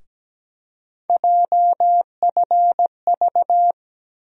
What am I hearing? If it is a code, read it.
Morse code, heard as a keyed tone, about 17 wpm, 700 Hz. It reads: JFV
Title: USA Call Sign Postfixes - 3 Letters 17wpm